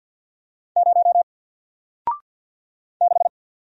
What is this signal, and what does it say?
Morse code, heard as a keyed tone, about 53 wpm, 700 Hz. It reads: 0 6
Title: Single Letter-Number 50wpm